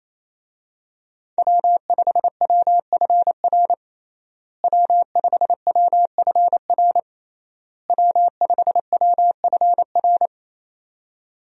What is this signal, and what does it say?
Morse code, heard as a keyed tone, about 28 wpm, 700 Hz. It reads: W5WFR W5WFR W5WFR
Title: USA Call Signs - 5 Characters - Repeated 3x - 28wpm